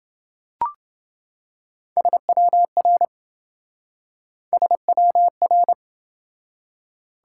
Morse code, heard as a keyed tone, about 26 wpm, 700 Hz. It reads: SWR SWR